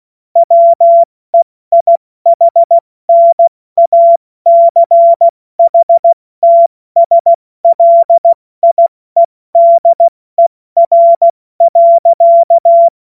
Morse code, heard as a keyed tone, about 16 wpm, spaced as for 14 wpm, 680 Hz. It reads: WEIHNACHTSLIEDER.
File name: Morsekurs / Witze / Witz 24. Januar 2023 70 F80